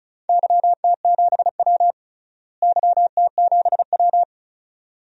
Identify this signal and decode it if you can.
Morse code, heard as a keyed tone, about 34 wpm, 700 Hz. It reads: YT7W YT7W
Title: DXCC-100 Call Signs - 4 Characters - Repeated 2x - 35wpm